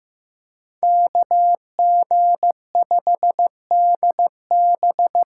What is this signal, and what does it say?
Morse code, heard as a keyed tone, about 15 wpm, 700 Hz. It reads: KG5DB
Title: USA Call Signs - 6 Characters - 8x Character Spacing - 15wpm